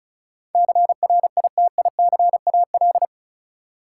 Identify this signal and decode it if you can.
Morse code, heard as a keyed tone, about 36 wpm, 700 Hz. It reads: CRITICAL
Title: Sentences from Top 1100 Words - New Word Drill Alt - 35wpm